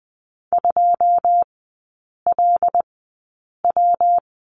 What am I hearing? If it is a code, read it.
Morse code, heard as a keyed tone, about 20 wpm, spaced as for 8 wpm, 690 Hz. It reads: 2LW
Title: 205 Lesson 33e - Letter j - Three Characters 20wpm